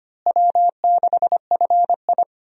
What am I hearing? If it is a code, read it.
Morse code, heard as a keyed tone, about 25 wpm, 700 Hz. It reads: W6FI
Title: USA Call Signs - 4 Characters - Repeated 3x - 25wpm